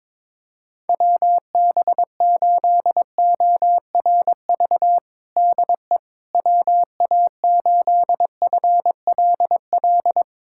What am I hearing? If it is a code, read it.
Morse code, heard as a keyed tone, about 22 wpm, 700 Hz. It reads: WB8ORV DE WA8FLL
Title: USA Callsign DE Callsign - 6 Characters - 22wpm